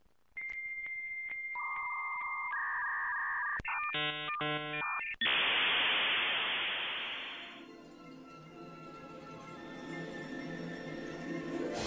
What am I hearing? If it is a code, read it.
Dial-up modem handshake.